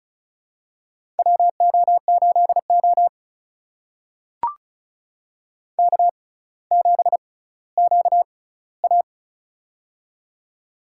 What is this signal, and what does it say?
Morse code, heard as a keyed tone, about 36 wpm, spaced as for 18 wpm, 700 Hz. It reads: WO8O K7QA
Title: USA Call Signs - 4 Characters - 6x Character Spacing - 35wpm